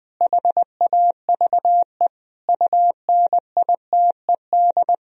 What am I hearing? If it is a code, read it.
Morse code, heard as a keyed tone, about 20 wpm, 700 Hz. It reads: HAVE UNITED
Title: Sentences from Top 1000 Words - Review Part 4 - 20wpm